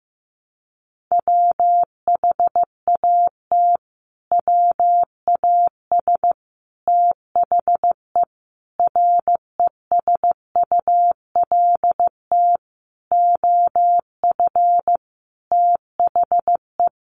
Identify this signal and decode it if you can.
Morse code, heard as a keyed tone, about 15 wpm, 700 Hz. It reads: WHAT WAS THE RESULT OF THE
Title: Sentences from Top 1500 Words 15wpm